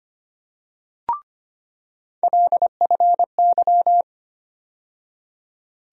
Morse code, heard as a keyed tone, about 25 wpm, 700 Hz. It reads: LFY
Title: USA Call Sign Postfixes - 3 Letters 25wpm